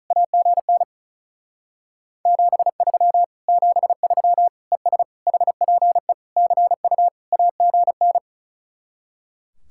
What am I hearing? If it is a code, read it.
Morse code, heard as a keyed tone, about 33 wpm, 700 Hz. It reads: AGN 73 73 ES HPE CU AGN
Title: Speed-Racing Common QSO Phrases - 35wpm